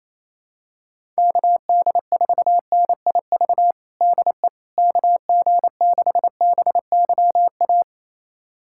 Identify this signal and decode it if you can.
Morse code, heard as a keyed tone, about 28 wpm, 700 Hz. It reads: KD4NIV DE KG6BYA
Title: USA Callsign DE Callsign 28wpm